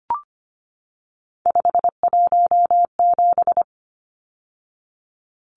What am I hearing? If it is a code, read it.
Morse code, heard as a keyed tone, about 25 wpm, 700 Hz. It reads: 517
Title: Numbers - 3-Digit 25wpm